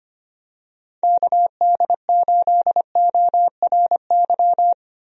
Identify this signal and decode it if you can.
Morse code, heard as a keyed tone, about 25 wpm, 700 Hz. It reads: KD8ORY
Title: USA Call Signs - 6 Characters - Repeated 2x - 25wpm